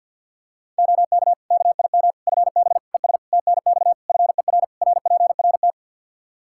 Morse code, heard as a keyed tone, about 52 wpm, 700 Hz. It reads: OK KIM FB ES TNX FER RPRT
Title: Practice QSO - Exchange 2 - Info - 50wpm